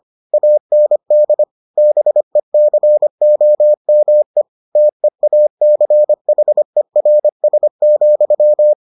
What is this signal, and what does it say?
Morse code, heard as a keyed tone, about 25 wpm, 600 Hz. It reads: AND BECOME TEACHERS,